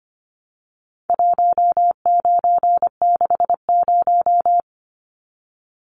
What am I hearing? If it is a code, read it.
Morse code, heard as a keyed tone, about 25 wpm, 700 Hz. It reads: 1960
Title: Numbers - 4-Digit 25wpm